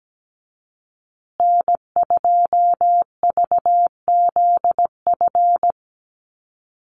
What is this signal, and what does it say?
Morse code, heard as a keyed tone, about 17 wpm, 700 Hz. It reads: N2VZF
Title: USA Call Signs Repeated 3x 17wpm